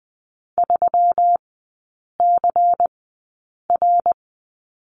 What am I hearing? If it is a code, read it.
Morse code, heard as a keyed tone, about 20 wpm, spaced as for 8 wpm, 700 Hz. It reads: 3CR
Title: 166 Lesson 27e - Number 7 - Three Characters 20wpm